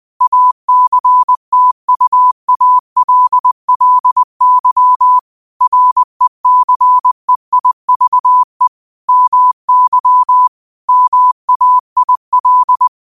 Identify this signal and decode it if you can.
Morse code, heard as a keyed tone, about 20 wpm, 1000 Hz. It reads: ACTUALLY RECEIVE MY MAIL